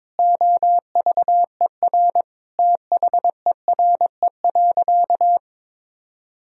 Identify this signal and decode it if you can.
Morse code, heard as a keyed tone, about 22 wpm, 700 Hz. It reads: OVER THERE.
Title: Sentences from Top 1000 Words - Review Part 2 - 22wpm